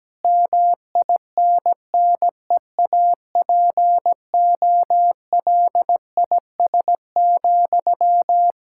Morse code, heard as a keyed tone, about 17 wpm, 700 Hz. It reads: MINNEAPOLIS,
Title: Practice QSO - Exchange 1 - Beginning - 17wpm